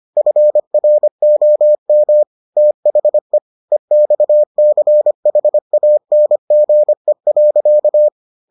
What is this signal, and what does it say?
Morse code, heard as a keyed tone, about 25 wpm, 600 Hz. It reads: FROM THE EXCHANGE.